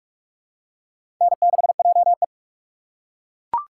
Morse code, heard as a keyed tone, about 42 wpm, 700 Hz. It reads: N6JE
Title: USA Call Signs - 4 Characters - 2x Character Spacing - 45wpm